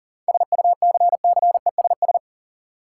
Morse code, heard as a keyed tone, about 40 wpm, 700 Hz. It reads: SUCCESS
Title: Sentences from Top 900 Words - New Word Drill - 40wpm